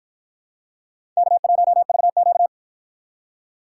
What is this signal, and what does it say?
Morse code, heard as a keyed tone, about 49 wpm, 700 Hz. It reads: K1VQ